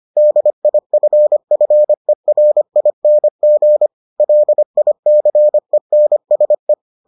Morse code, heard as a keyed tone, about 25 wpm, 600 Hz. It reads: DIFFERING LICENSE